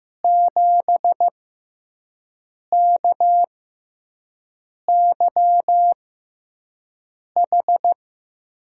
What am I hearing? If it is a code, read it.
Morse code, heard as a keyed tone, about 15 wpm, spaced as for 5 wpm, 700 Hz. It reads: 7KYH
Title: USA Call Signs - 6x Character Spacing - 15wpm